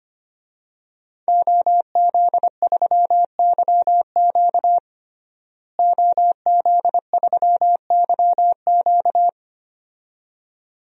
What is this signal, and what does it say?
Morse code, heard as a keyed tone, about 25 wpm, 700 Hz. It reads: OZ3YQ OZ3YQ
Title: DXCC-100 Call Signs - 5 Characters - Repeated 2x - 25wpm